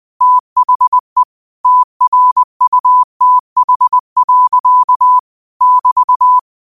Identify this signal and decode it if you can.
Morse code, heard as a keyed tone, about 20 wpm, 1000 Hz. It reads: THE TRUTH. <BT>
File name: 20wpm